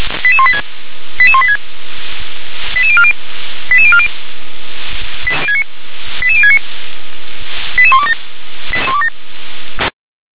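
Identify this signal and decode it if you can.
Five-tone selective call (ZVEI), heard as quick runs of five tones, tones of about 70 ms. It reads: 88186 88186 88040 88040 88069 88127
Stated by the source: ZVEI five-tone selective call received by SDR.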